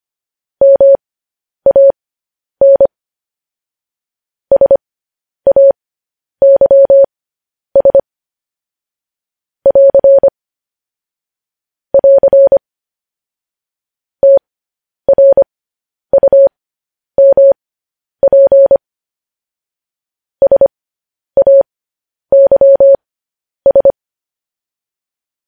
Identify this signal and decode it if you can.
Morse code, heard as a keyed tone, about 25 wpm, spaced as for 10 wpm, 550 Hz. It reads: MAN SAYS <AR> <AR> TRUMP SAYS
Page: news-headlines-in-morse-code-at-10-wpm